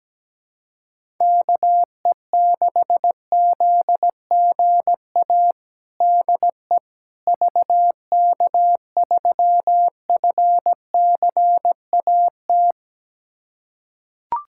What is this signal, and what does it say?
Morse code, heard as a keyed tone, about 17 wpm, 700 Hz. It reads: KE6ZGA DE VK3FCAT
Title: USA Callsign DE Callsign 17wpm